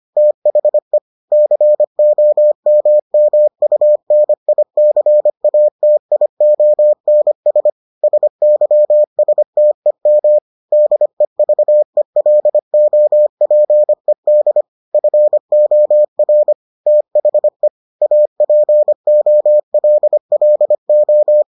Morse code, heard as a keyed tone, about 25 wpm, 600 Hz. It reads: THE COMMUNICATIONS SYSTEM DEVELOPED FOR THE APOLLO